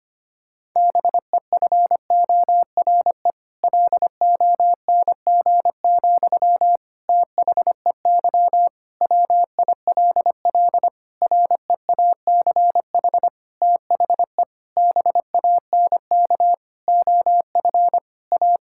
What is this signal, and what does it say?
Morse code, heard as a keyed tone, about 25 wpm, 700 Hz. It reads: BEFORE LONG, THEY WILL REACH THE BANK OF A
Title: Sentences from Top 1000 Words - Review Part 6 - 25wpm